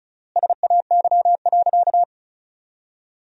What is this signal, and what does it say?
Morse code, heard as a keyed tone, about 34 wpm, 700 Hz. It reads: SAY.
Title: 3x Word-Spacing - Sentences from Top 500 Words - 35wpm